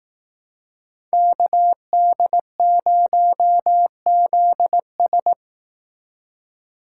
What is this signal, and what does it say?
Morse code, heard as a keyed tone, about 18 wpm, 700 Hz. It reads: KD0ZS
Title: Speed Racing - USA 2x2 Call Signs 15wpm